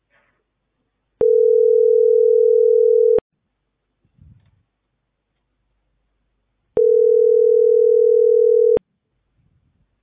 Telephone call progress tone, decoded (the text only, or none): ringback tone